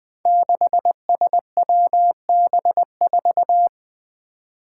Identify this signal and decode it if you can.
Morse code, heard as a keyed tone, about 21 wpm, 700 Hz. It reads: 6SWB4
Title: Five Challenging Letter-Number Combinations 20wpm